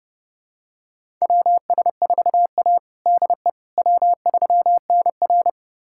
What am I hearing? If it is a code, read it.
Morse code, heard as a keyed tone, about 32 wpm, 700 Hz. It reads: WS4A DE W3NR
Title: USA Callsign DE Callsign - 4 Characters - 30wpm